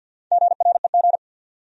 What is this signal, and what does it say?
Morse code, heard as a keyed tone, about 52 wpm, 700 Hz. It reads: GREG
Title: USA Common Names - 50wpm